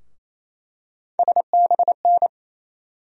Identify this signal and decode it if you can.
Morse code, heard as a keyed tone, about 27 wpm, 700 Hz. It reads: SBN
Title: USA Call Sign Postfixes - 3 Letters 28wpm